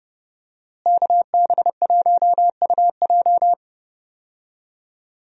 Morse code, heard as a keyed tone, about 30 wpm, 700 Hz. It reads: KB1UJ
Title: USA Call Signs - 5 Characters - Repeated 3x - 30wpm